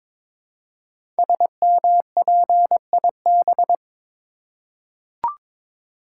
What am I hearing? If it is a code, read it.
Morse code, heard as a keyed tone, about 22 wpm, 700 Hz. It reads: SMPIB
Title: Code Groups - 5 Letters - 22wpm